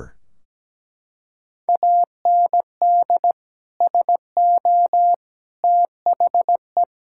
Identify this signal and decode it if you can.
Morse code, heard as a keyed tone, about 17 wpm, 700 Hz. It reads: AND SO THE
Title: Sentences from Top 1600 Words 17wpm